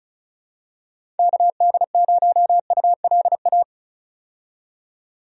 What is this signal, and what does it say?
Morse code, heard as a keyed tone, about 35 wpm, 700 Hz. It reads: KD0ULA
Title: USA Call Signs - 6 Characters - 35wpm